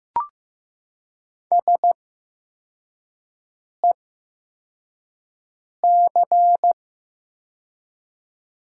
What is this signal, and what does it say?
Morse code, heard as a keyed tone, about 15 wpm, spaced as for 4 wpm, 700 Hz. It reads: SEC
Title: 8x Character Spacing - Top 500 Words - 15wpm